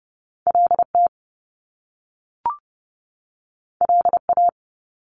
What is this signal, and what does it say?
Morse code, heard as a keyed tone, about 30 wpm, 700 Hz. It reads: LT LA
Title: Top Two Letter Combinations in Words 30wpm